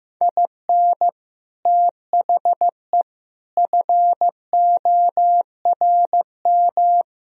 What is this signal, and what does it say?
Morse code, heard as a keyed tone, about 15 wpm, 700 Hz. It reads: IN THE FORM
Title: Speed-Racing - Sets of 4 Words 15wpm